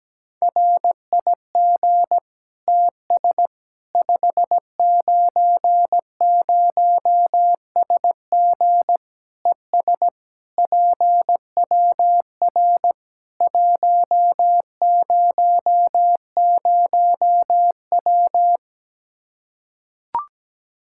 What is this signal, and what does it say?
Morse code, heard as a keyed tone, about 17 wpm, 700 Hz. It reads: RIG TS 590SG ES PWR 100W
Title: Practice QSO - Exchange 2 - Info - 17wpm